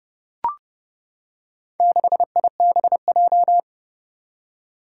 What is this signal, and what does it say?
Morse code, heard as a keyed tone, about 30 wpm, 700 Hz. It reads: 6IBJ
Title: Four Challenging Letter-Number Combinations 30wpm